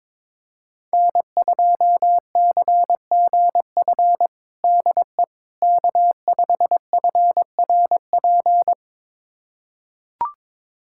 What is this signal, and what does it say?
Morse code, heard as a keyed tone, about 21 wpm, 700 Hz. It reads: N2CGF DE K5FRP
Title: USA Callsign DE Callsign - 5 Characters - 22wpm